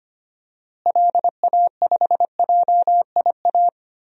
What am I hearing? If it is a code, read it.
Morse code, heard as a keyed tone, about 25 wpm, 700 Hz. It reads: LA5JIA
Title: DXCC-100 Call Signs - 6 Characters - Repeated 2x - 25wpm